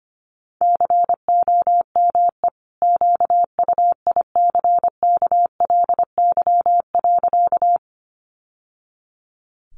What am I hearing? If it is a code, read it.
Morse code, heard as a keyed tone, about 25 wpm, 700 Hz. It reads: COME QUICKLY.
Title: Sentences from Top 700 Words 25wpm